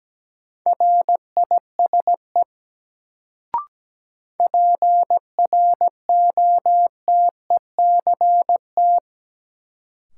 Morse code, heard as a keyed tone, about 17 wpm, 700 Hz. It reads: RISE PROTECT
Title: Sentences from Top 900 Words - New Word Drill - 17wpm